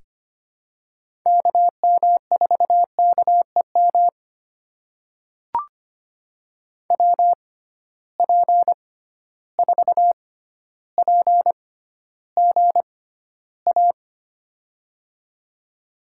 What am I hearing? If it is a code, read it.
Morse code, heard as a keyed tone, about 25 wpm, spaced as for 13 wpm, 700 Hz. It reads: KM4KEM WP4PGA
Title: USA Call Signs - 6 Characters - 6x Character Spacing - 25wpm